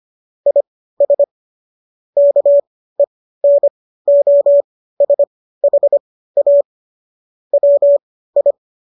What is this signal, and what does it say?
Morse code, heard as a keyed tone, about 25 wpm, spaced as for 15 wpm, 570 Hz. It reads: IS KENOSHA WI
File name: qso211 15